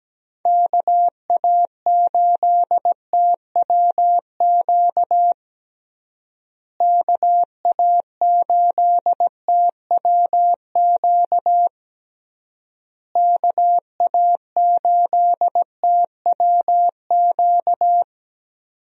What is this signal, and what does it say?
Morse code, heard as a keyed tone, about 17 wpm, 700 Hz. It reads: KA8TWQ KA8TWQ KA8TWQ
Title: USA Call Signs - 6 Characters - Repeated 3x - 17wpm